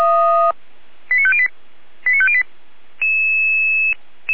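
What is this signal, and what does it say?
Five-tone selective call (ZVEI), heard as quick runs of five tones, tones of about 70 ms. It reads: 89508 89508
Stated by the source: ZVEI five-tone call received by SDR, fire brigade Orlamünde (feuerwehr-orlamuende).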